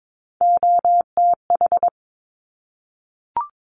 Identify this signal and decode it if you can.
Morse code, heard as a keyed tone, about 22 wpm, 700 Hz. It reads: OTH
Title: Top Three Letter Combinations in Words 22wpm